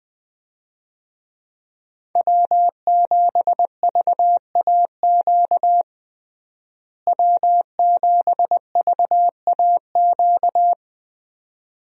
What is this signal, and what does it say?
Morse code, heard as a keyed tone, about 20 wpm, 700 Hz. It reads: W7VAQ W7VAQ
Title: USA Call Signs - 5 Characters - Repeated 2x - 20wpm